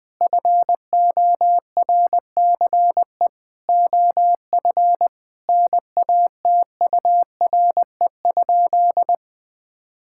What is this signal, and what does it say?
Morse code, heard as a keyed tone, about 20 wpm, 700 Hz. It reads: FORCE OF NATURE?
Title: Sentences from Top 1000 Words - Review Part 5 - 20wpm